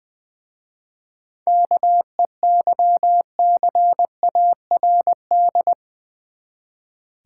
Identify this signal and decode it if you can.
Morse code, heard as a keyed tone, about 20 wpm, 690 Hz. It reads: KEYCARD
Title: Easy Words with Hints - 20wpm